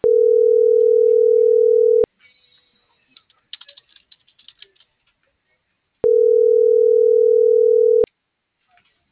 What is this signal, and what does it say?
Ringback tone (North America), 440 + 480 Hz.